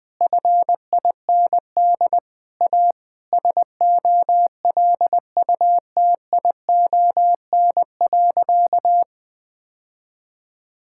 Morse code, heard as a keyed tone, about 20 wpm, 700 Hz. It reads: FIND A SOLUTION.
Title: Sentences with 8-Word Phrases - Spoken Beginning - 20wpm